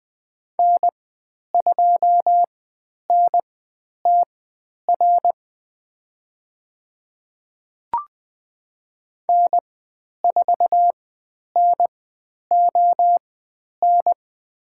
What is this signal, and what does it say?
Morse code, heard as a keyed tone, about 20 wpm, spaced as for 10 wpm, 700 Hz. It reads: N2NTR N4NON
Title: 102 Lesson 17g - Letter u - Call Signs 20wpm